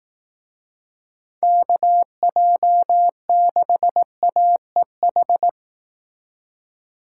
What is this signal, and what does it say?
Morse code, heard as a keyed tone, about 18 wpm, 700 Hz. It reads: KJ6AEH
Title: Speed Racing - USA 2x3 Call Signs 15wpm